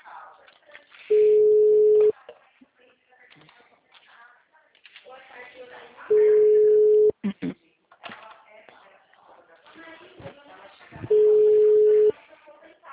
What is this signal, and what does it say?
Ringback tone (Europe, 425 Hz), with speech.
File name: out- -2010- - - .90297